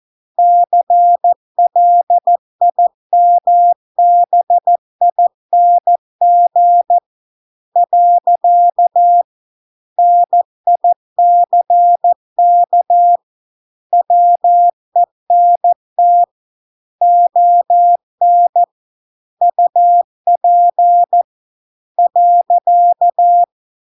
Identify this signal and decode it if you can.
Morse code, heard as a keyed tone, about 14 wpm, 700 Hz. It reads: CLIMBING . NICK WENT ON UP .